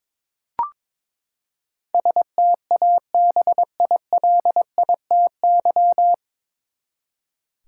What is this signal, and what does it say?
Morse code, heard as a keyed tone, about 22 wpm, 700 Hz. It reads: STABILITY
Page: Morse Code Ninja practice 22wpm